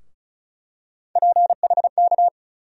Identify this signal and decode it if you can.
Morse code, heard as a keyed tone, about 36 wpm, 700 Hz. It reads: PHK
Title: USA Call Sign Postfixes - 3 Letters 35wpm